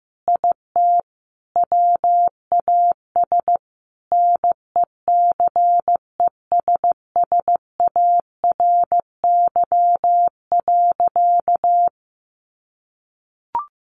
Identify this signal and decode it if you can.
Morse code, heard as a keyed tone, about 15 wpm, 700 Hz. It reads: IT WAS NECESSARY.
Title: Sentences from Top 800 Words 15wpm